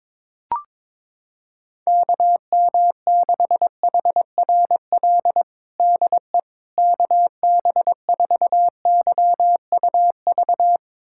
Morse code, heard as a keyed tone, about 22 wpm, 700 Hz. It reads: KM6HRL DE KB4YUV
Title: USA Callsign DE Callsign 22wpm